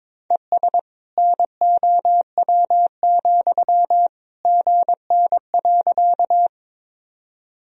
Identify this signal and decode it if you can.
Morse code, heard as a keyed tone, about 22 wpm, 700 Hz. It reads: ES NOW, GN.
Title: Sentences from Top 300 Words and Abbreviations 22wpm